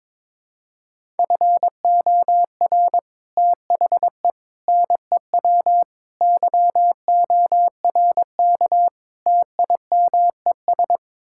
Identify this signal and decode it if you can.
Morse code, heard as a keyed tone, about 22 wpm, 700 Hz. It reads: FOR THE NEW YORK TIMES
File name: practice / 22wpm / Build A Phrase - 5-Word Phrases - 22wpm